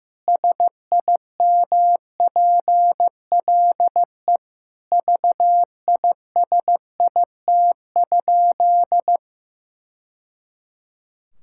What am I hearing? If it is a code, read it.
Morse code, heard as a keyed tone, about 15 wpm, 700 Hz. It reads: SIMPLE VISIT?
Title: Sentences from Top 1000 Words - Review Part 6 - 15wpm